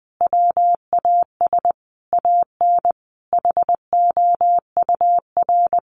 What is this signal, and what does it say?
Morse code, heard as a keyed tone, about 20 wpm, 700 Hz. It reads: WAS AN HOUR
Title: Sentences from Top 1300 Words 20wpm